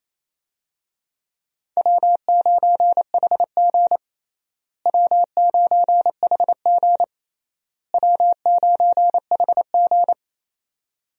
Morse code, heard as a keyed tone, about 28 wpm, 700 Hz. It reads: W9HG W9HG W9HG
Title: USA Call Signs - 4 Characters - Repeated 3x - 28wpm